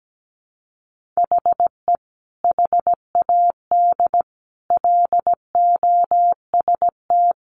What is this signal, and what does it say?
Morse code, heard as a keyed tone, about 17 wpm, 700 Hz. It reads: HE HAD LOST
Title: Sentences from Top 1400 Words 17wpm